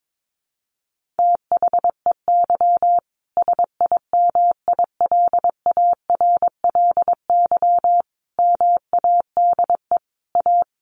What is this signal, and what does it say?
Morse code, heard as a keyed tone, about 22 wpm, 700 Hz. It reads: THEY SIMILARLY MADE A
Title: Sentences from Top 1400 Words 22wpm